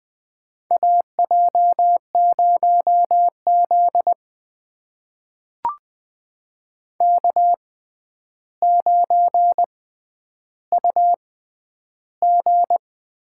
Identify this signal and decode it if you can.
Morse code, heard as a keyed tone, about 21 wpm, spaced as for 10 wpm, 700 Hz. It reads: AJ0Z K9UG
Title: USA Call Signs - 4 Characters - 6x Character Spacing - 20wpm